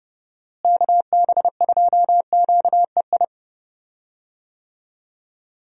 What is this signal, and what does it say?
Morse code, heard as a keyed tone, about 30 wpm, 700 Hz. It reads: KB2QEI
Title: USA Call Signs - 6 Characters - Repeated 2x - 30wpm